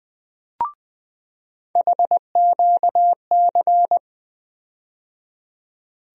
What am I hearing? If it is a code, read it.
Morse code, heard as a keyed tone, about 20 wpm, 700 Hz. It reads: HQC
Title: USA Call Sign Postfixes - 3 Letters 20wpm